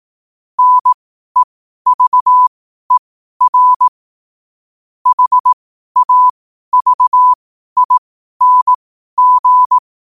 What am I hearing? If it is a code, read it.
Morse code, heard as a keyed tone, about 18 wpm, spaced as for 12 wpm, 1000 Hz. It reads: NEVER HAVING
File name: qotdfiles / 12wpm